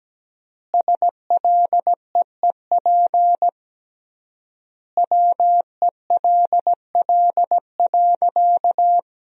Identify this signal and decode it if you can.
Morse code, heard as a keyed tone, about 17 wpm, 700 Hz. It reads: SLEEP WELL.
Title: 3x Word-Spacing - Sentences from Top 500 Words - 17wpm